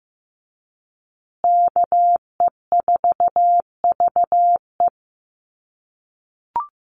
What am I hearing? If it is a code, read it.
Morse code, heard as a keyed tone, about 15 wpm, 700 Hz. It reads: KE4VE
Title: USA Call Signs Repeated 3x 15wpm